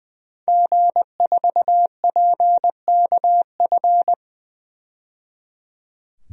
Morse code, heard as a keyed tone, about 21 wpm, 700 Hz. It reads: G4PKF
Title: DXCC-100 Call Signs - 5 Characters - 20wpm